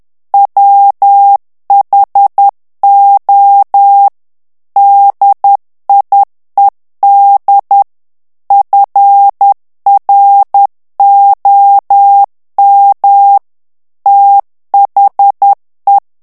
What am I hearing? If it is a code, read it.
Morse code, heard as a keyed tone, about 10 wpm, 800 Hz. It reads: WHO DIED FROM THE
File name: morse mp3 / ANGLAIS texte clair 2 vitesse 10 mots minute